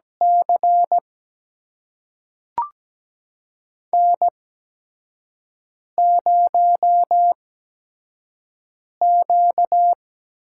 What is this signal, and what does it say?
Morse code, heard as a keyed tone, about 17 wpm, spaced as for 5 wpm, 700 Hz. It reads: C N0Q
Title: USA Call Signs - 5 Characters - 8x Character Spacing - 17wpm